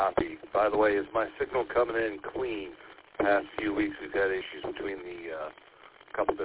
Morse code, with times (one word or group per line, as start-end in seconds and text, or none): none